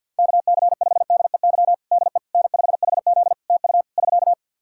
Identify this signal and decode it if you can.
Morse code, heard as a keyed tone, about 47 wpm, 700 Hz. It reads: KC5DEY DE N5HZ TU <SK>
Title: Practice QSO - Exchange 4 - Ending - 50wpm